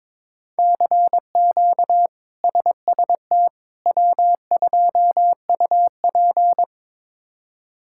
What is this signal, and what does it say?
Morse code, heard as a keyed tone, about 22 wpm, 700 Hz. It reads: CQ SST W2UP
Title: K1USN SST QSO Practice - 22wpm